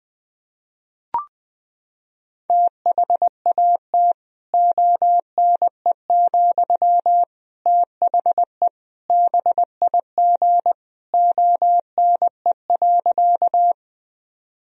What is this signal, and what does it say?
Morse code, heard as a keyed tone, about 20 wpm, 700 Hz. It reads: THAT ONE, THE BIG ONE.
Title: Sentences from Top 1000 Words - Review Part 1 - 20wpm